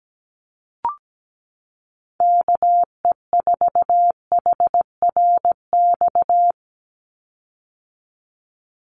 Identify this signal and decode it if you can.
Morse code, heard as a keyed tone, about 17 wpm, 700 Hz. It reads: KE4HRX
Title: USA Call Signs 17wpm